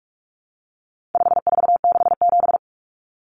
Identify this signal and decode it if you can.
Morse code, heard as a keyed tone, about 44 wpm, 700 Hz. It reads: 5467
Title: Common Cut Numbers - 4-Digit 45wpm